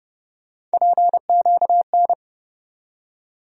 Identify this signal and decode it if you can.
Morse code, heard as a keyed tone, about 30 wpm, 700 Hz. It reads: PQN
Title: USA Call Sign Postfixes - 3 Letters 30wpm